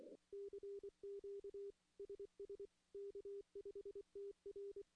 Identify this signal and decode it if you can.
Morse code, heard as a keyed tone, about 24 wpm, 400 Hz. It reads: CQ SS K5TR
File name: CW / CW-SS-k5tr- -1700